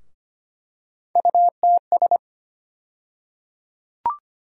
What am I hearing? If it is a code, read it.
Morse code, heard as a keyed tone, about 25 wpm, 700 Hz. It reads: UTS
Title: USA Call Sign Postfixes - 3 Letters 25wpm